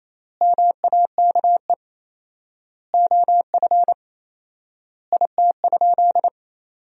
Morse code, heard as a keyed tone, about 28 wpm, 700 Hz. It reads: MAKE OF IT?
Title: 4x Word-Spacing - Sentences from Top 100 Words - 28wpm